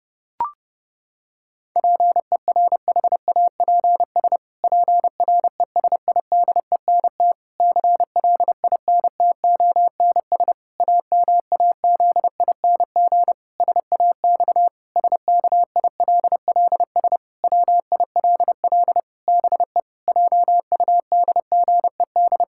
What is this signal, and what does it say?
Morse code, heard as a keyed tone, about 30 wpm, 700 Hz. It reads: PERHAPS PRESIDENT CLINTONS AMAZING SAX SKILLS WILL BE JUDGED
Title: Pangram Challenge v2 - 30wpm